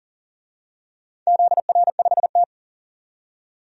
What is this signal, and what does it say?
Morse code, heard as a keyed tone, about 41 wpm, 700 Hz. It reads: ZR5T
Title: DXCC-100 Call Signs - 4 Characters - 40wpm